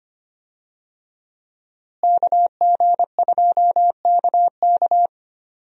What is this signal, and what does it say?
Morse code, heard as a keyed tone, about 25 wpm, 700 Hz. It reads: KG2KK
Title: USA Call Signs - 5 Characters - Repeated 3x - 25wpm